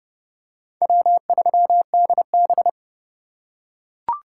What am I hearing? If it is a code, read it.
Morse code, heard as a keyed tone, about 30 wpm, 700 Hz. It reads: W3DB
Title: USA Call Signs - 4 Characters - 2x Character Spacing - 30wpm